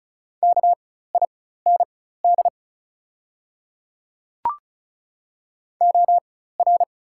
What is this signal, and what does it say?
Morse code, heard as a keyed tone, about 36 wpm, spaced as for 16 wpm, 690 Hz. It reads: KIND OR
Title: 4x Character Spacing - Top 200 Words - 35wpm